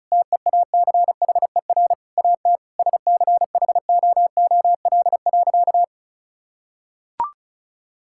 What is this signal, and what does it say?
Morse code, heard as a keyed tone, about 36 wpm, 700 Hz. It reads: TEACHER AT SCHOOL.
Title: Sentences from Top 1000 Words - Review Part 6 - 35wpm